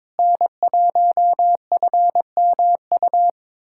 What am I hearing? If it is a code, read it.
Morse code, heard as a keyed tone, about 22 wpm, 700 Hz. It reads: N1FMU
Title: USA Callsign DE Callsign - 5 Characters - 22wpm